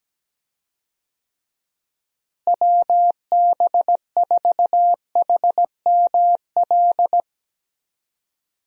Morse code, heard as a keyed tone, about 17 wpm, 700 Hz. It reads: WB4HML
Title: USA Call Signs - 6 Characters - Repeated 2x - 17wpm